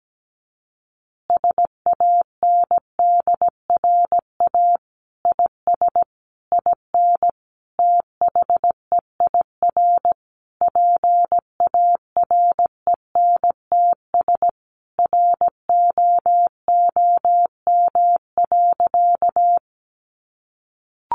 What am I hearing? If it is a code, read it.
Morse code, heard as a keyed tone, about 17 wpm, 700 Hz. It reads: SANDRA IS IN THEIR PARENTS ROOM.
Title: Easy Phrases 17wpm